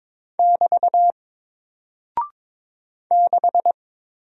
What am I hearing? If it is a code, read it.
Morse code, heard as a keyed tone, about 22 wpm, 700 Hz. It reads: <BT> 6
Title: Single Character (Letter:Number:Punctuation:Prosign) Alt 22wpm